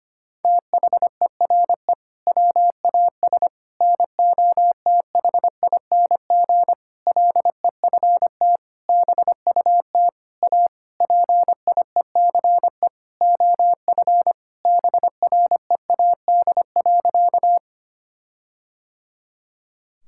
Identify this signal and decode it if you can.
Morse code, heard as a keyed tone, about 25 wpm, 700 Hz. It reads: THERE WAS NOTHING LEFT BUT A PIECE OF BREAD.
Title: Sentences from Top 1600 Words 25wpm